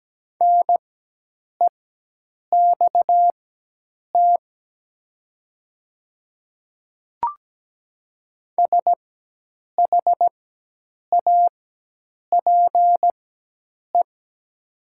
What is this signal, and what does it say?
Morse code, heard as a keyed tone, about 17 wpm, spaced as for 8 wpm, 700 Hz. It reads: NEXT SHAPE
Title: 4x Character Spacing - Top 500 Words - 17wpm